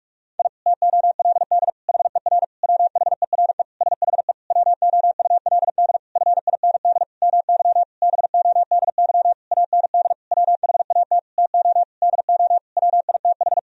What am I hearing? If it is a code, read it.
Morse code, heard as a keyed tone, about 46 wpm, 700 Hz. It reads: I TOLD HER WHERE SHE WOULD FIND MY BODY AND WHAT TO DO WITH
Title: Sentences from Top 1000 Words - Review Part 2 - 45wpm